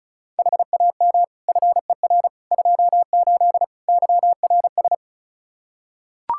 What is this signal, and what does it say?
Morse code, heard as a keyed tone, about 36 wpm, 700 Hz. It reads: HAM FER 28 YRS
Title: K1USN SST QSO Practice 35wpm